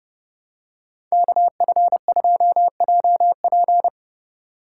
Morse code, heard as a keyed tone, about 30 wpm, 700 Hz. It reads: KF2JP